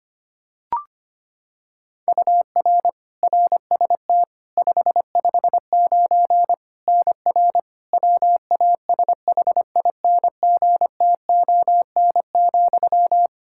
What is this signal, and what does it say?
Morse code, heard as a keyed tone, about 25 wpm, 700 Hz. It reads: UR RST 559 NR WASHINGTON,
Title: K1USN SST QSO Practice 25wpm